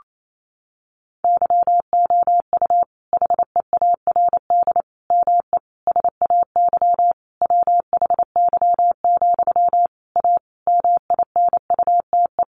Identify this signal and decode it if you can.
Morse code, heard as a keyed tone, about 28 wpm, 700 Hz. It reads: YOU HEARD ME SAY WHY, A MINUTE
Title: Sentences from Top 500 Words 28wpm